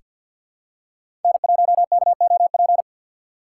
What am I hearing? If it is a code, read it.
Morse code, heard as a keyed tone, about 47 wpm, 700 Hz. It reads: N1KOP
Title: USA Call Signs - 5 Characters - 6x Character Spacing - 50wpm